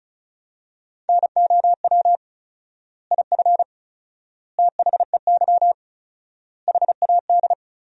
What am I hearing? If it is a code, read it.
Morse code, heard as a keyed tone, about 36 wpm, 700 Hz. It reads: NOW IF THEY HAD
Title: 4x Word-Spacing - Sentences from Top 500 Words - 35wpm